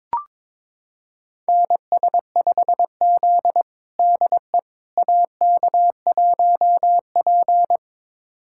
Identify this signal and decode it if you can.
Morse code, heard as a keyed tone, about 22 wpm, 700 Hz. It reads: NS5Z DE AK1P
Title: USA Callsign DE Callsign - 4 Characters - 22wpm